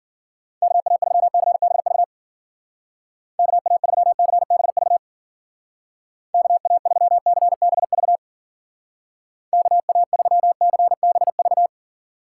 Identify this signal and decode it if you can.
Morse code, heard as a keyed tone, about 52 wpm, 700 Hz. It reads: KA3CBV KA3CBV KA3CBV KA3CBV